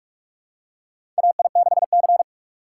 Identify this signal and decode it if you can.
Morse code, heard as a keyed tone, about 46 wpm, 700 Hz. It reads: AI6C
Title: USA Call Signs - 4 Characters - 2x Character Spacing - 45wpm